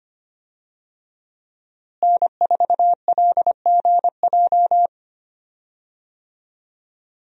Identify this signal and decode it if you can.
Morse code, heard as a keyed tone, about 25 wpm, 700 Hz. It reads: N4LGJ
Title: USA Call Signs - 5 Characters - Repeated 3x - 25wpm